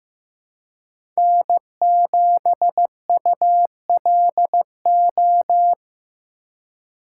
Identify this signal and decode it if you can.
Morse code, heard as a keyed tone, about 15 wpm, 700 Hz. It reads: N7ULO
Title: USA Call Signs - 5 Characters - 3x Character Spacing - 15wpm